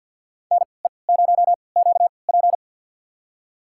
Morse code, heard as a keyed tone, about 49 wpm, spaced as for 28 wpm, 700 Hz. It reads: NE0QP
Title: USA Call Signs - 5 Characters - 3x Character Spacing - 50wpm